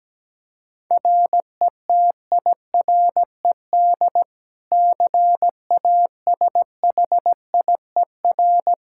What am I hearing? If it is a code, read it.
Morse code, heard as a keyed tone, about 17 wpm, 700 Hz. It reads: RETIRED CASHIER
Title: Practice QSO - Exchange 3 - More Info - 17wpm